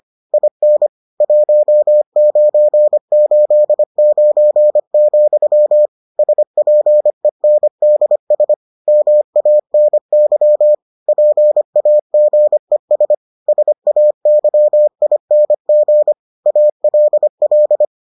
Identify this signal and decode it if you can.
Morse code, heard as a keyed tone, about 25 wpm, 600 Hz. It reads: IN 1989, SPENDS MANY PAGES SAYING ALL